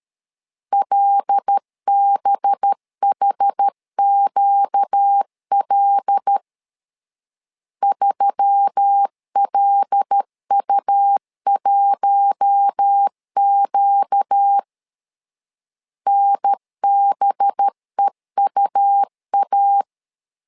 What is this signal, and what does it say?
Morse code, heard as a keyed tone, about 13 wpm, 800 Hz. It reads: LBHQL 3LU1Q NBEUA
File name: LEZIONI / 5 LBVC / 5 LB 12wpm 1906